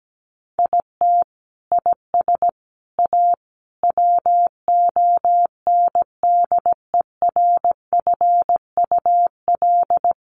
Morse code, heard as a keyed tone, about 17 wpm, 700 Hz. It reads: IT IS A WONDERFUL
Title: Sentences from Top 1300 Words 17wpm